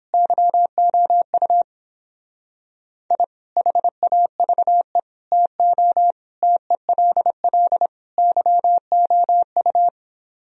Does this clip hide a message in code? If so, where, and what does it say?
Yes: Morse code from 0.1 to 9.9 s, heard as a keyed tone, about 29 wpm, 700 Hz. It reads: YOU I HAVE TO TELL YOU